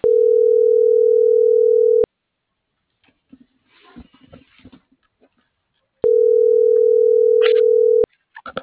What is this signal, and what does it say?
Ringback tone (North America), 440 + 480 Hz.